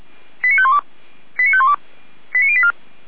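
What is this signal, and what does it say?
Five-tone selective call (ZVEI), heard as quick runs of five tones, tones of about 70 ms. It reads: 89512 89512 89084
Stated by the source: ZVEI five-tone selective call received on SDR.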